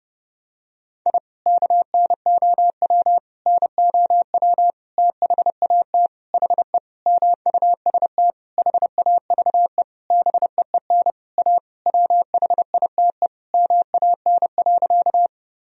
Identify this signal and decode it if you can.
Morse code, heard as a keyed tone, about 30 wpm, 700 Hz. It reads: I KNOW NOW THAT HE MUST HAVE BEEN A WHITE MAN.
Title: Sentences from Top 1000 Words - Review Part 1 - 30wpm